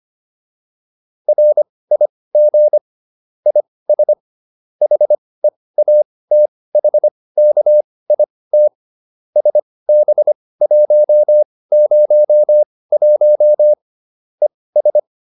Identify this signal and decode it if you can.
Morse code, heard as a keyed tone, about 25 wpm, spaced as for 18 wpm, 600 Hz. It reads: RIG IS HEATHKIT SB101 ES